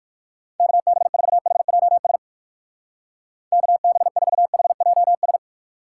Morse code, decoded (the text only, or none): KB4HJS KB4HJS